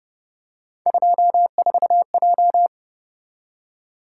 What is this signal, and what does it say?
Morse code, heard as a keyed tone, about 29 wpm, 700 Hz. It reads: 24J
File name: Speed-Racing - Three Letter-Number Combinations - 25wpm